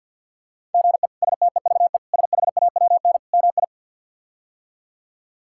Morse code, heard as a keyed tone, about 53 wpm, 700 Hz. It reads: GE STEVE SHAWN MI
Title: K1USN SST QSO Practice - 50wpm